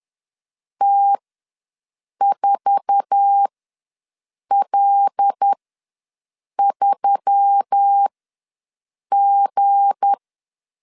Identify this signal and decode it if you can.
Morse code, heard as a keyed tone, about 10 wpm, spaced as for 6 wpm, 800 Hz. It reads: T4L3G
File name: LEZIONI / 8 YJ45 / 8 YJ4 ws 10wpm 4020